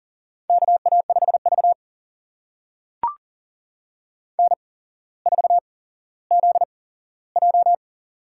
Morse code, heard as a keyed tone, about 40 wpm, spaced as for 17 wpm, 700 Hz. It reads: KA5V N4ZJ
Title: USA Call Signs - 4 Characters - 8x Character Spacing - 40wpm